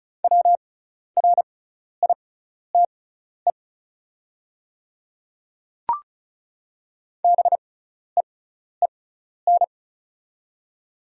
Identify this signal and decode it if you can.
Morse code, heard as a keyed tone, about 36 wpm, spaced as for 12 wpm, 700 Hz. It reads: WRITE BEEN
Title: 6x Character Spacing - Top 100 Words - 35wpm